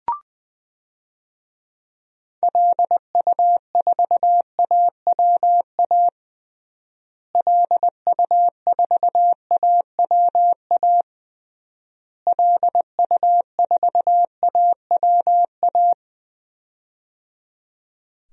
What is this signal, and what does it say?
Morse code, heard as a keyed tone, about 20 wpm, 700 Hz. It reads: LU4AWA LU4AWA LU4AWA
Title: DXCC-100 Call Signs - 6 Characters - Repeated 3x - 20wpm